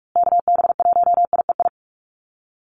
Morse code, heard as a keyed tone, about 44 wpm, 700 Hz. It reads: KB1IEI
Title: USA Call Signs Repeated 2x 45wpm